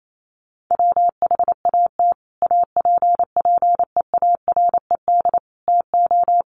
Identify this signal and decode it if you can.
Morse code, heard as a keyed tone, about 28 wpm, 700 Hz. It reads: WHAT APPEARED TO
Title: Sets of 5 Words Encore2 28wpm